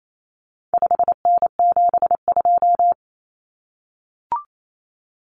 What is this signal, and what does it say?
Morse code, heard as a keyed tone, about 28 wpm, 700 Hz. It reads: 5N72
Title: Common Cut Numbers - 4-Digit 28wpm